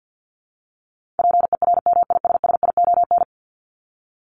Morse code, heard as a keyed tone, about 48 wpm, 700 Hz. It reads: PERMISSION
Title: Top 1501-1750 Words 50wpm